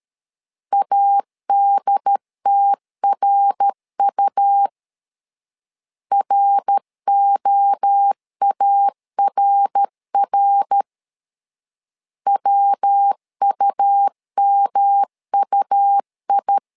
Morse code, heard as a keyed tone, about 13 wpm, 800 Hz. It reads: ADTRU ROARR WUMUI
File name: LEZIONI / 3 MWRO / 3 MWRO 12wpm 1730